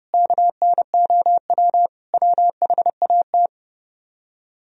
Morse code, heard as a keyed tone, about 33 wpm, 700 Hz. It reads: KNOW WHAT